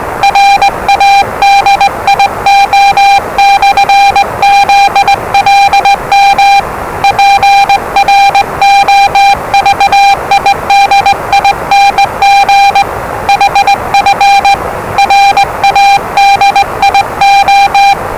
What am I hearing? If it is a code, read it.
Morse code, heard as a keyed tone, about 20 wpm, 800 Hz, with noise. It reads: RADIO/ZLM PROVIDING HF RADIO
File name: WellingtonRadioZlwCloseDownOnHfAndMf / 01-Zlw 4277khz 417.5 qrt until 0925z